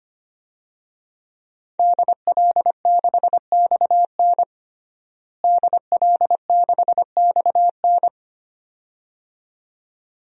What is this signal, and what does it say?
Morse code, heard as a keyed tone, about 25 wpm, 700 Hz. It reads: DL6XN DL6XN
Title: DXCC-100 Call Signs - 5 Characters - Repeated 2x - 25wpm